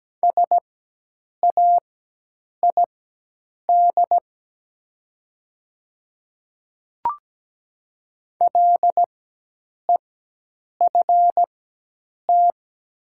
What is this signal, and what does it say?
Morse code, heard as a keyed tone, about 17 wpm, spaced as for 8 wpm, 690 Hz. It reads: SAID LEFT
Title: 4x Character Spacing - Top 300 Words - 17wpm